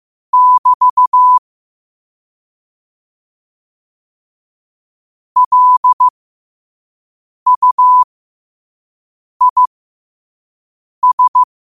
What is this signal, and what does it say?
Morse code, heard as a keyed tone, about 15 wpm, spaced as for 5 wpm, 1000 Hz. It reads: <BT> LUIS